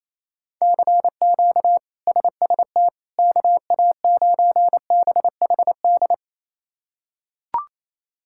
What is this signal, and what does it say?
Morse code, heard as a keyed tone, about 28 wpm, 700 Hz. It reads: CQ SST KA9BHD
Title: K1USN SST QSO Practice - 28wpm